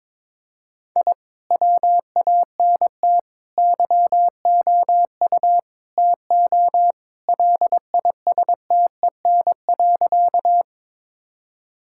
Morse code, heard as a keyed tone, about 21 wpm, 700 Hz. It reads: I WANT YOU TO LISTEN.
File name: practice / 22wpm / Sentences with 4-Word Phrases - 22wpm